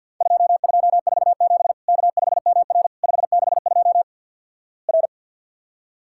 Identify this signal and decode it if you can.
Morse code, heard as a keyed tone, about 50 wpm, 700 Hz. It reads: 2247 K5MR 562 R
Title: Worked All Europe DX Contest - QTC Exchange with Repeats - 50wpm